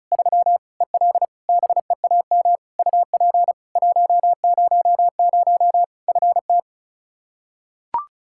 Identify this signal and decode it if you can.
Morse code, heard as a keyed tone, about 36 wpm, 700 Hz. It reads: 3 EL BEAM UP 100 FT
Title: K1USN SST QSO Practice 35wpm